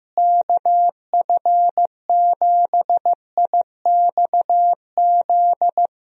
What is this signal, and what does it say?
Morse code, heard as a keyed tone, about 15 wpm, 700 Hz. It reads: KF7IXZ
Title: USA Call Signs - 6 Characters - Repeated 2x - 15wpm